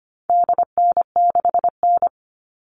Morse code, heard as a keyed tone, about 25 wpm, 700 Hz. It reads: DN6N
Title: All Cut Numbers - 4-Digit 25wpm